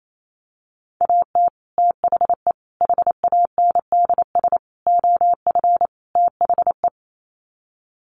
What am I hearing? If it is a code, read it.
Morse code, heard as a keyed tone, about 28 wpm, 700 Hz. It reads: AT THE HANDS OF THE
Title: Sets of 5 Words Encore2 28wpm